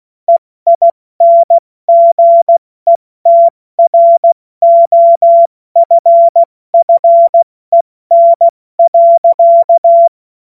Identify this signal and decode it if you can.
Morse code, heard as a keyed tone, about 16 wpm, spaced as for 14 wpm, 680 Hz. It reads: EINGETROFFEN.
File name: Morsekurs / Witze / Witz 17. Mai 2022 70 F80